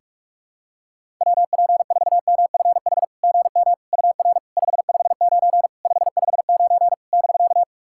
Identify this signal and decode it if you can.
Morse code, heard as a keyed tone, about 47 wpm, 700 Hz. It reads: WP4MFH GM UR 559 559 <BK>
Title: SOTA QSO Practice 45wpm